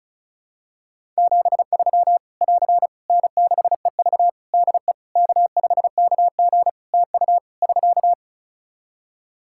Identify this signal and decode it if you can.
Morse code, heard as a keyed tone, about 36 wpm, 700 Hz. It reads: 73 <AR> N6EV DE K5KG TU <SK>
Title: K1USN SST QSO Practice 35wpm